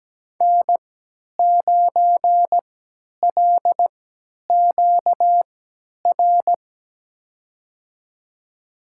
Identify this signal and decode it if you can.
Morse code, heard as a keyed tone, about 17 wpm, spaced as for 10 wpm, 700 Hz. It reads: N9LQR
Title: USA Call Signs - 3x Character Spacing - 17wpm